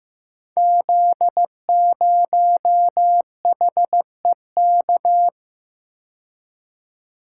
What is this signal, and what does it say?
Morse code, heard as a keyed tone, about 15 wpm, 700 Hz. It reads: Z0HEK
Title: Code Groups - 5 Letter:Numbers - 15wpm